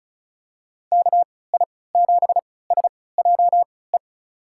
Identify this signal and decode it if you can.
Morse code, heard as a keyed tone, about 36 wpm, spaced as for 20 wpm, 700 Hz. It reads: KI7SJE
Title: USA Call Signs - 6 Characters - 3x Character Spacing - 35wpm